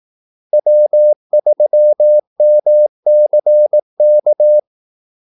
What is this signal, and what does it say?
Morse code, heard as a keyed tone, about 18 wpm, 600 Hz. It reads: W3MCK
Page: Kaldesignaler 18 wpm